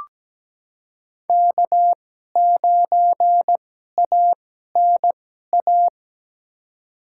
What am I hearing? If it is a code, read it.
Morse code, heard as a keyed tone, about 17 wpm, spaced as for 12 wpm, 700 Hz. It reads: K9ANA
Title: USA Call Signs - 5 Characters - 2x Character Spacing - 17wpm